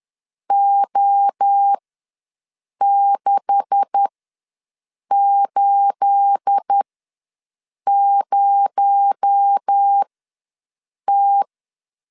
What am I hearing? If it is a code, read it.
Morse code, heard as a keyed tone, about 10 wpm, spaced as for 6 wpm, 800 Hz. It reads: O680T
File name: LEZIONI / 10 SIGNS / 10C DOT ws 10wpm 4213